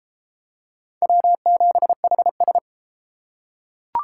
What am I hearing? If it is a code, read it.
Morse code, heard as a keyed tone, about 29 wpm, 700 Hz. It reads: W7HS